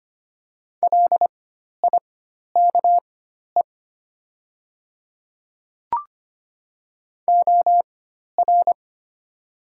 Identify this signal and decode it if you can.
Morse code, heard as a keyed tone, about 25 wpm, spaced as for 12 wpm, 700 Hz. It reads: LIKE OR
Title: 4x Character Spacing - Top 100 Words - 25wpm